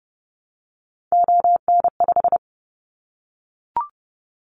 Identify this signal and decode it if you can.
Morse code, heard as a keyed tone, about 30 wpm, 700 Hz. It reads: ON5
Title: Callsign Prefixes for Top 25 DXCC - Repeated 3x - 30wpm